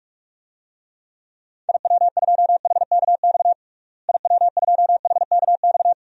Morse code, heard as a keyed tone, about 46 wpm, 700 Hz. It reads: IW2HKX IW2HKX
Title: DXCC-100 Call Signs - 6 Characters - Repeated 3x - 45wpm